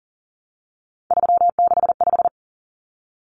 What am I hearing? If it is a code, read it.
Morse code, heard as a keyed tone, about 40 wpm, 700 Hz. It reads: 365
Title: Common Cut Numbers - 3-Digit 40wpm